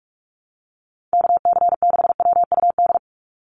Morse code, heard as a keyed tone, about 44 wpm, 690 Hz. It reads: KC6WUD
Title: USA Call Signs 45wpm